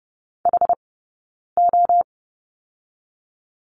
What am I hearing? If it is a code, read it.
Morse code, heard as a keyed tone, about 30 wpm, 700 Hz. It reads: H O
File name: Lesson 17d - Letter u - Two Characters 30wpm